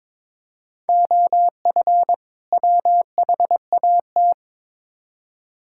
Morse code, heard as a keyed tone, about 22 wpm, 700 Hz. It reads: OF WHAT
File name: Build A Phrase - 2-Word Phrases - 22wpm